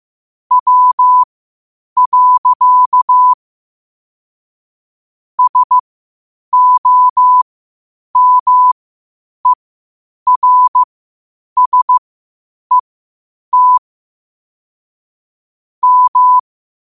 Morse code, heard as a keyed tone, about 15 wpm, spaced as for 8 wpm, 1000 Hz. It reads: W. SOMERSET M
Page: New quotes every day in morse code at 8 Words per minute.